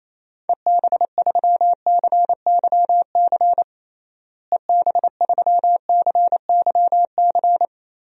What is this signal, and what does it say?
Morse code, heard as a keyed tone, about 28 wpm, 700 Hz. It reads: EB3CYC EB3CYC
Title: DXCC-100 Call Signs - 6 Characters - Repeated 3x - 28wpm